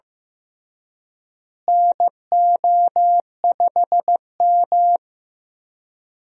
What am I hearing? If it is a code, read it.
Morse code, heard as a keyed tone, about 15 wpm, 700 Hz. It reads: NO5M
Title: USA Call Signs - 4 Characters - Repeated 3x - 15wpm